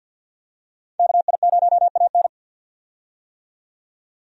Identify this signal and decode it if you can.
Morse code, heard as a keyed tone, about 51 wpm, 700 Hz. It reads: KI0AN
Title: USA Call Signs - 5 Characters - 50wpm